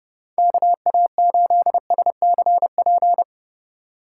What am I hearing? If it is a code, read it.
Morse code, heard as a keyed tone, about 38 wpm, 700 Hz. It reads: KA8SCP